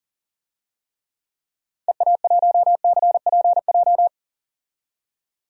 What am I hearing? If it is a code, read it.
Morse code, heard as a keyed tone, about 40 wpm, 700 Hz. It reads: EA1CPJ
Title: DXCC-100 Call Signs - 6 Characters - 40wpm